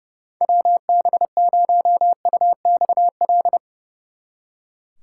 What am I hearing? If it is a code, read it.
Morse code, heard as a keyed tone, about 34 wpm, 700 Hz. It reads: WB0UXL
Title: Speed Racing - USA 2x3 Call Signs 30wpm